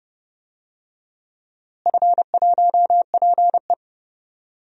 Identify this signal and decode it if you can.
Morse code, heard as a keyed tone, about 30 wpm, 700 Hz. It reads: F1PE
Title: DXCC-100 Call Signs - 4 Characters - Repeated 3x - 30wpm